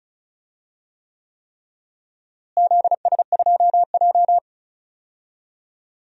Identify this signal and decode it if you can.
Morse code, heard as a keyed tone, about 36 wpm, 700 Hz. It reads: ZS2J
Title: DXCC-100 Call Signs - 4 Characters - Repeated 3x - 35wpm